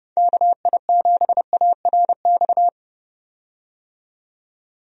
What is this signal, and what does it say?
Morse code, heard as a keyed tone, about 30 wpm, 700 Hz. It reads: KI7ARX
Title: USA Call Signs - 6 Characters - 30wpm